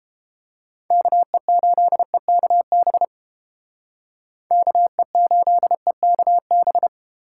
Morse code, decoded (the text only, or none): KE8EKB KE8EKB